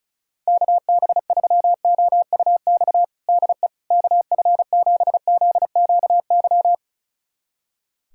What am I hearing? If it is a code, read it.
Morse code, heard as a keyed tone, about 34 wpm, 700 Hz. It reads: KB3OUX DE KF7ZQY
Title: USA Callsign DE Callsign - 6 Characters - 35wpm